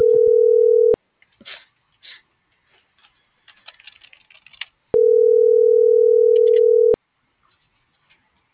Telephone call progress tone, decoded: ringback tone

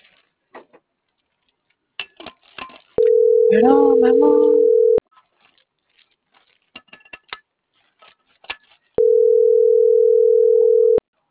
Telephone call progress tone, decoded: ringback tone

